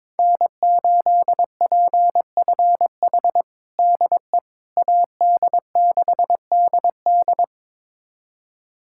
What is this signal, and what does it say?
Morse code, heard as a keyed tone, about 21 wpm, 700 Hz. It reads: N8PFH DE AD6DD
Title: USA Callsign DE Callsign - 5 Characters - 22wpm